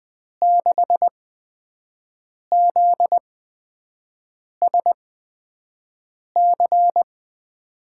Morse code, heard as a keyed tone, about 20 wpm, spaced as for 5 wpm, 700 Hz. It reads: 6ZSC
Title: USA Call Signs - 6 Characters - 8x Character Spacing - 20wpm